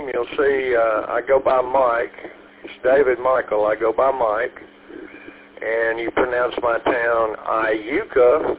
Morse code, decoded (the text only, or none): none